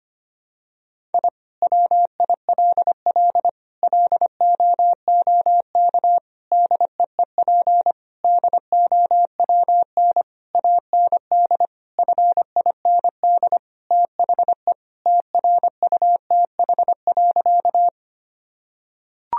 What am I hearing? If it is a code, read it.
Morse code, heard as a keyed tone, about 25 wpm, 700 Hz. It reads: I WILL LOOK DEEP DOWN AND FIND THE TRUTH.
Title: Sentences from Top 1000 Words - Review Part 3 - 25wpm